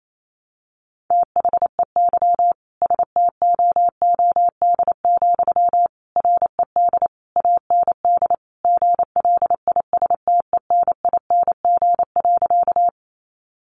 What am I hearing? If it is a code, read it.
Morse code, heard as a keyed tone, about 28 wpm, 700 Hz. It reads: THEY STOOD, RED AND GLISTENING.
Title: A Sound Of Thunder 28wpm